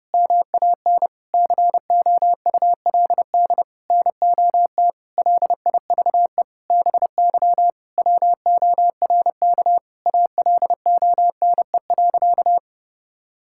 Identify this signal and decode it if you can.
Morse code, heard as a keyed tone, about 32 wpm, 700 Hz. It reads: MAN COULD NOT LIVE BY WORK ALONE.
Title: Sentences from Top 1600 Words 30wpm